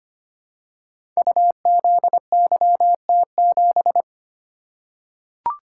Morse code, heard as a keyed tone, about 25 wpm, 700 Hz. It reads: UZYT7
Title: Code Groups - 5 Letter:Numbers - 25wpm